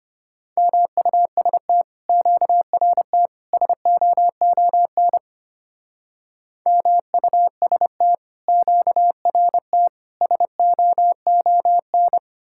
Morse code, heard as a keyed tone, about 31 wpm, 700 Hz. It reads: MUST QRT SOON MUST QRT SOON